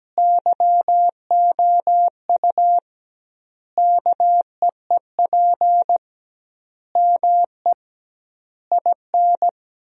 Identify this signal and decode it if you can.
Morse code, heard as a keyed tone, about 17 wpm, 700 Hz. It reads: YOU KEEP ME IN
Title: 2x Word-Spacing - Sentences from Top 500 Words - 17wpm